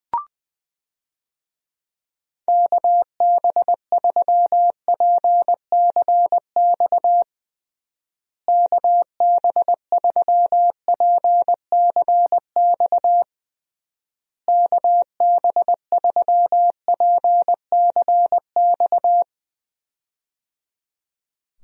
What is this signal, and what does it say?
Morse code, heard as a keyed tone, about 20 wpm, 700 Hz. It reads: KB3PCX KB3PCX KB3PCX
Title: USA Call Signs - 6 Characters - Repeated 3x - 20wpm